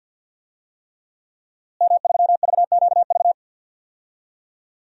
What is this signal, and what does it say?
Morse code, heard as a keyed tone, about 52 wpm, 700 Hz. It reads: M3VQV
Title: DXCC-100 Call Signs - 5 Characters - Repeated 3x - 50wpm